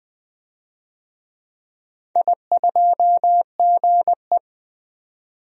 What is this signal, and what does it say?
Morse code, heard as a keyed tone, about 20 wpm, 700 Hz. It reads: I2GE
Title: DXCC-100 Call Signs - 4 Characters - 20wpm